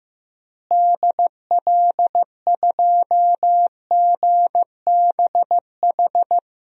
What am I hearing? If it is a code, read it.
Morse code, heard as a keyed tone, about 16 wpm, 700 Hz. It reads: DL2GBH
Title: DXCC-100 Call Signs - 6 Characters - Repeated 3x - 15wpm